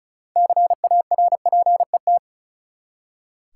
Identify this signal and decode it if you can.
Morse code, heard as a keyed tone, about 36 wpm, 700 Hz. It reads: CARPET
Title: Six Letter Words - 35wpm